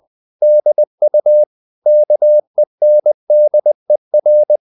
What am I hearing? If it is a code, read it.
Morse code, heard as a keyed tone, about 20 wpm, 600 Hz. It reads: DU KENDER
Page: Ferie 20 wpm